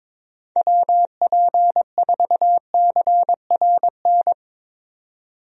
Morse code, heard as a keyed tone, about 22 wpm, 700 Hz. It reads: WP4CRN
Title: USA Call Signs - 6 Characters - 22wpm